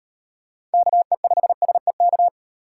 Morse code, heard as a keyed tone, about 36 wpm, 700 Hz. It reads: KE5SEK